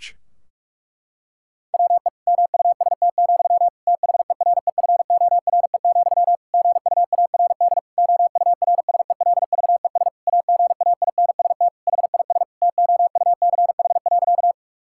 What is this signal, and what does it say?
Morse code, heard as a keyed tone, about 43 wpm, 700 Hz. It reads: WE MUST, THEREFORE, GUARD OURSELVES AGAINST HIS TOUCH.
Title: Sentences from Top 1600 Words 45wpm